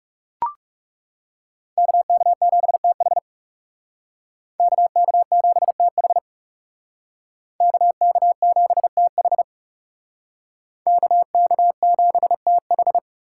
Morse code, heard as a keyed tone, about 38 wpm, 700 Hz. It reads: KK7TH KK7TH KK7TH KK7TH